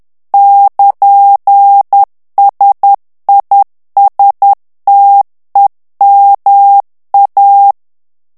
Morse code, heard as a keyed tone, about 10 wpm, 800 Hz. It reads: <KN>SISTEMA
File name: morse mp3 / ITALIEN texte clair 1 vitesse 10 mots minute